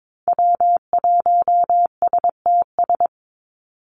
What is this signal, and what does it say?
Morse code, heard as a keyed tone, about 22 wpm, 700 Hz. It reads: W1STS
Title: USA Call Signs 22wpm